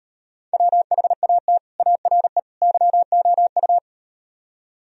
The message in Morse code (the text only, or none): WHAT ARE YOU